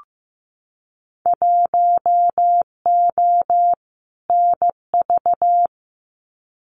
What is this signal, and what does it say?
Morse code, heard as a keyed tone, about 15 wpm, 700 Hz. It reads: 1O NV
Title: Winter Field Day Exchanges Encore 15wpm